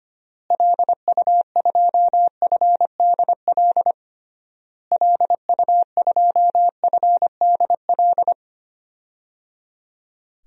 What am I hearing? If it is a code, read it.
Morse code, heard as a keyed tone, about 25 wpm, 700 Hz. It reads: LU2FDL LU2FDL
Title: DXCC-100 Call Signs - 6 Characters - Repeated 3x - 25wpm